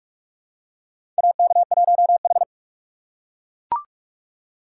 Morse code, heard as a keyed tone, about 42 wpm, 700 Hz. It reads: AK1H
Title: USA Call Signs - 4 Characters - 4x Character Spacing - 45wpm